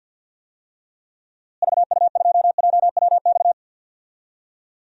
Morse code, heard as a keyed tone, about 47 wpm, 700 Hz. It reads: VU2JWX
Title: DXCC-100 Call Signs - 6 Characters - Repeated 2x - 50wpm